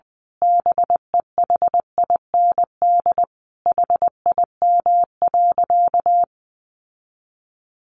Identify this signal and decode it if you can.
Morse code, heard as a keyed tone, about 20 wpm, 700 Hz. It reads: BEHIND HIM.
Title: Sentences from Top 1000 Words 20wpm